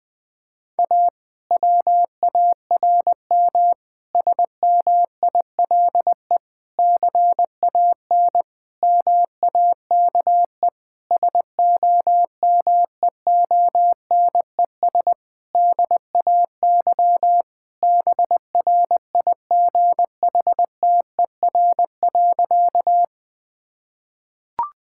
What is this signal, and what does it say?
Morse code, heard as a keyed tone, about 20 wpm, 700 Hz. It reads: A WARM SMILE CAN MAKE SOMEONES DAY BRIGHTER.
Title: Difficult Sentences with Hints - 20wpm